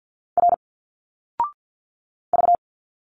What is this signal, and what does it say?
Morse code, heard as a keyed tone, about 50 wpm, 700 Hz. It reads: R V
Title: Single Letters 50wpm